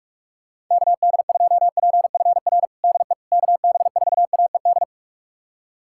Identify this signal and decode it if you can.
Morse code, heard as a keyed tone, about 46 wpm, 700 Hz. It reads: KD2PFR DE KB4AED
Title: USA Callsign DE Callsign - 6 Characters - 45wpm